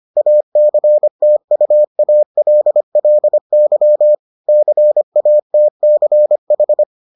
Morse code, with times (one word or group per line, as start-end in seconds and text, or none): 0.2-4.1 ACTUALLY
4.5-6.8 CATCH